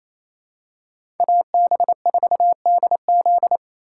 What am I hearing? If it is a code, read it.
Morse code, heard as a keyed tone, about 28 wpm, 700 Hz. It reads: AB4DZ
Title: USA Call Signs - 5 Characters - 2x Character Spacing - 28wpm